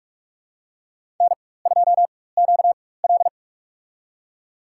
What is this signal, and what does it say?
Morse code, heard as a keyed tone, about 47 wpm, spaced as for 21 wpm, 700 Hz. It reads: N2QL
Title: USA Call Signs - 4x Character Spacing - 45wpm